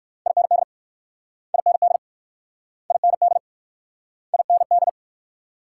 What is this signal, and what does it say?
Morse code, heard as a keyed tone, about 56 wpm, 700 Hz. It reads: IND IND IND IND
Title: Speed-Racing Multi-letter Phonemes - 45wpm